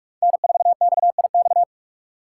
Morse code, heard as a keyed tone, about 41 wpm, 700 Hz. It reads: N4XIX
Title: USA Call Signs - 2x Character Spacing - 45wpm